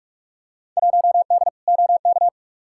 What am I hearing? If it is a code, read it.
Morse code, heard as a keyed tone, about 45 wpm, 700 Hz. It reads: 1D OK
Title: Field Day Exchanges - 45wpm